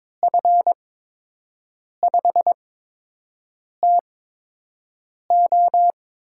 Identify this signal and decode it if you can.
Morse code, heard as a keyed tone, about 22 wpm, spaced as for 6 wpm, 700 Hz. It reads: F5TO
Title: USA Call Signs - 5 Characters - 8x Character Spacing - 22wpm